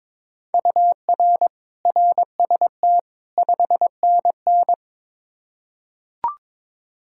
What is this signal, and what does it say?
Morse code, heard as a keyed tone, about 22 wpm, 700 Hz. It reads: UR RST 5NN
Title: K1USN SST QSO Practice 22wpm